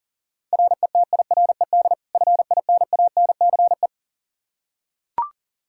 Morse code, heard as a keyed tone, about 40 wpm, 700 Hz. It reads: RETIRED FINANCE
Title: Practice QSO - Exchange 3 - More Info - 40wpm